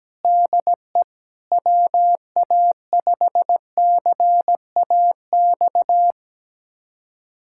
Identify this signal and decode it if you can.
Morse code, heard as a keyed tone, about 17 wpm, 700 Hz. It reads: DE WA5CAX
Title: USA Callsign DE Callsign - 6 Characters - 17wpm